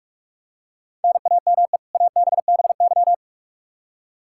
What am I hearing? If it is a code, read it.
Morse code, heard as a keyed tone, about 46 wpm, 700 Hz. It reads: NAME ABBY
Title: Practice QSO - Exchange 1 - Beginning - 45wpm